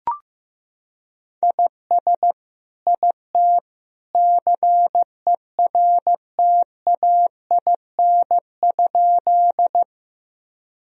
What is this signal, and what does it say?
Morse code, heard as a keyed tone, about 15 wpm, 700 Hz. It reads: IS IT CERTAIN?
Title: Sentences from Top 1000 Words - Review Part 2 - 15wpm